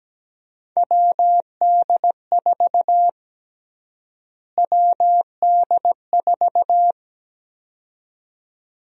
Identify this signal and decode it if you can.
Morse code, heard as a keyed tone, about 17 wpm, 700 Hz. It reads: WD4 WD4
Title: USA Call Sign Prefixes - Repeated 2x - 17wpm